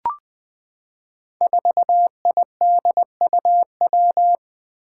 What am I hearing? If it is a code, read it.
Morse code, heard as a keyed tone, about 20 wpm, 700 Hz. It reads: 4IDUW
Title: Five Challenging Letter-Number Combinations 20wpm